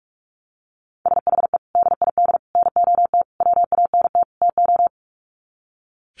Morse code, heard as a keyed tone, about 44 wpm, 700 Hz. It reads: SHE DID NOT WANT TO
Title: Sets of 5 Words 45wpm